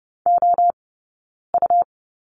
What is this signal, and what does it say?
Morse code, heard as a keyed tone, about 30 wpm, spaced as for 9 wpm, 690 Hz. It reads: OU
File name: Lesson 19d - Letter w - Two Characters 30wpm